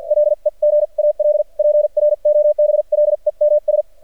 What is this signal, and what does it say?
Morse code, heard as a keyed tone, about 45 wpm, spaced as for 39 wpm, 600 Hz. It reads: LEGAL PROBLEMS
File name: archived / CRSnet